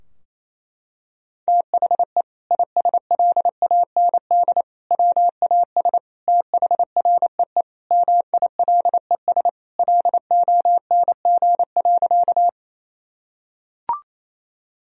Morse code, heard as a keyed tone, about 28 wpm, 700 Hz. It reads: THE ISLAND WAS THREE MILES LONG.
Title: Sentences from Top 1000 Words - Review Part 3 - 28wpm